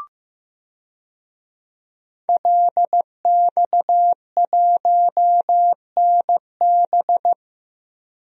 Morse code, heard as a keyed tone, about 15 wpm, 700 Hz. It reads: LX1NB
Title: DXCC-100 Call Signs - 5 Characters - 15wpm